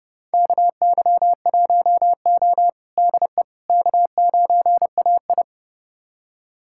Morse code, heard as a keyed tone, about 30 wpm, 700 Hz. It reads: KY1O DE K9AI
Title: USA Callsign DE Callsign - 4 Characters - 30wpm